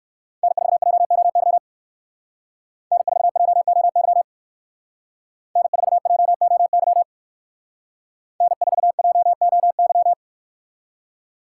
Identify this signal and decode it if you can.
Morse code, heard as a keyed tone, about 63 wpm, 700 Hz. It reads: N4JOY N4JOY N4JOY N4JOY